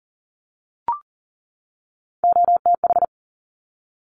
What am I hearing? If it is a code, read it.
Morse code, heard as a keyed tone, about 40 wpm, 700 Hz. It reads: OTH
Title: Top Three Letter Combinations in Words 40wpm